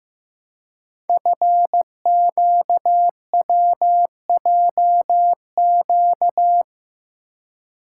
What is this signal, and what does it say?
Morse code, heard as a keyed tone, about 15 wpm, 700 Hz. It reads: FQWJQ
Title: Code Groups - 5 Letters - 15wpm